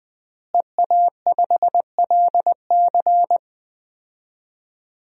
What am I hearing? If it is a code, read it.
Morse code, heard as a keyed tone, about 20 wpm, 700 Hz. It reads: EA5LC
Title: DXCC-100 Call Signs - 5 Characters - Repeated 2x - 20wpm